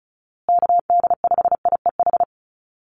Morse code, heard as a keyed tone, about 34 wpm, 700 Hz. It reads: KD5IEH
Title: USA Call Signs Repeated 2x 35wpm